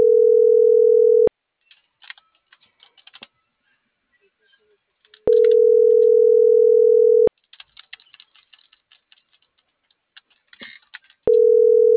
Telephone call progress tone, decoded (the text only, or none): ringback tone